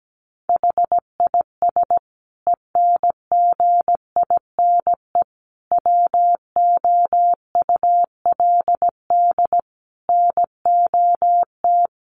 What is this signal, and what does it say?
Morse code, heard as a keyed tone, about 17 wpm, 700 Hz. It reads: HIS ENGINE WOULD NOT
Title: Sentences from Top 1500 Words 17wpm